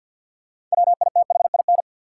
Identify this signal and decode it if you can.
Morse code, heard as a keyed tone, about 52 wpm, 700 Hz. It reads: WITHIN
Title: Sets of 2 Words - New Word Drill - 50wpm